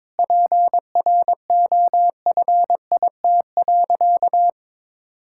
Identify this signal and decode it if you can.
Morse code, heard as a keyed tone, about 22 wpm, 700 Hz. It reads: PROFIT.
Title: Sentences from Top 1000 Words - Review Part 5 - 22wpm